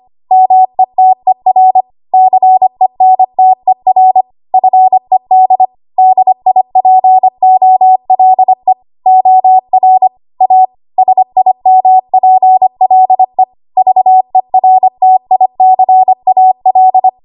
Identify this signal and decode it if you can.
Morse code, heard as a keyed tone, about 25 wpm, 750 Hz. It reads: METER CENTER FED DIPOLE OR A SIMPLE VERTICAL
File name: Archive / 25 WPM / 25WPM